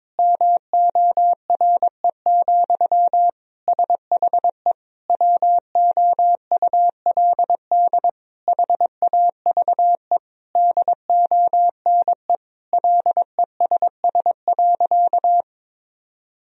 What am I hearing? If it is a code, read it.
Morse code, heard as a keyed tone, about 22 wpm, 700 Hz. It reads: MORE, SHE WOULD HAVE DONE LESS.
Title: Sentences from Top 1000 Words - Review Part 3 - 22wpm